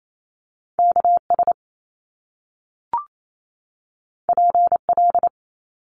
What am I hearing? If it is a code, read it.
Morse code, heard as a keyed tone, about 28 wpm, 700 Hz. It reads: KS PL
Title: Top Two Letter Combinations in Words 28wpm